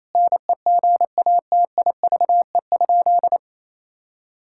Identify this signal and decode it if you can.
Morse code, heard as a keyed tone, about 28 wpm, 700 Hz. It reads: NEGATIVE?
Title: Sentences from Top 1000 Words - Review Part 6 - 28wpm